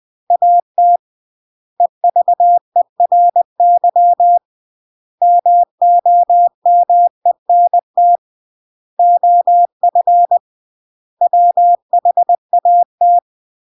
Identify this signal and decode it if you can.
Morse code, heard as a keyed tone, about 20 wpm, 700 Hz. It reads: AT EVERY MOMENT OF WHAT